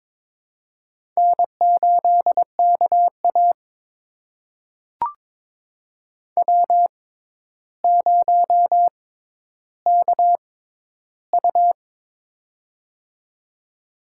Morse code, heard as a keyed tone, about 22 wpm, spaced as for 11 wpm, 700 Hz. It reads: N8KA W0KU
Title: USA Call Signs - 4 Characters - 6x Character Spacing - 22wpm